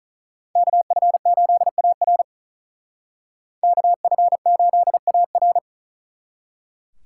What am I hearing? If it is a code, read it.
Morse code, heard as a keyed tone, about 44 wpm, 700 Hz. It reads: KF8AR KF8AR